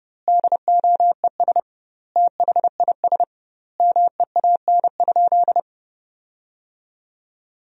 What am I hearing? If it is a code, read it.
Morse code, heard as a keyed tone, about 32 wpm, 700 Hz. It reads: DOES THIS MEAN?
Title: 2x Word-Spacing - Sentences from Top 200 Words - 30wpm